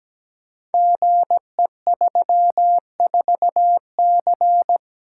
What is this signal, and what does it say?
Morse code, heard as a keyed tone, about 17 wpm, 700 Hz. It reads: GE34C
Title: Code Groups - 5 Letter:Numbers - 17wpm